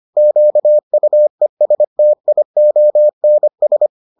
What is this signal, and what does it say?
Morse code, heard as a keyed tone, about 25 wpm, 600 Hz. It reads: QUESTIONS